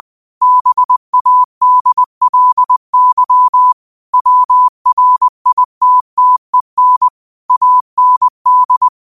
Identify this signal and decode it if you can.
Morse code, heard as a keyed tone, about 20 wpm, 1000 Hz. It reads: BADLY WRITTEN AND